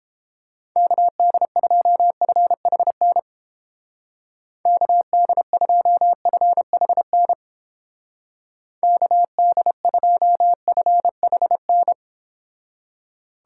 Morse code, heard as a keyed tone, about 28 wpm, 700 Hz. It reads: KD2FHN KD2FHN KD2FHN